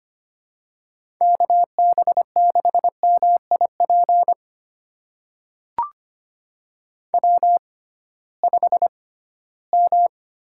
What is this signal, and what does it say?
Morse code, heard as a keyed tone, about 25 wpm, spaced as for 16 wpm, 700 Hz. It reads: KB6MIP W5M
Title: USA Call Signs - 6x Character Spacing - 25wpm